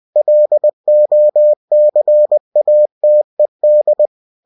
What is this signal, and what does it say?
Morse code, heard as a keyed tone, about 20 wpm, 600 Hz. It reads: LOCATED
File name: cw-books / winnie-the-pooh / winnie-20wpm-page-0114